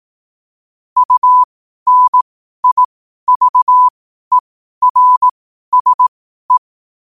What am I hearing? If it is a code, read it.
Morse code, heard as a keyed tone, about 18 wpm, spaced as for 12 wpm, 1000 Hz. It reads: UNIVERSE
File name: qotdfiles / 12wpm